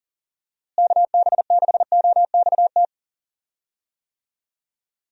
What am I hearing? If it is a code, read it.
Morse code, heard as a keyed tone, about 40 wpm, 700 Hz. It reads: KB6OXT
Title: USA Call Signs - 6 Characters - Repeated 2x - 40wpm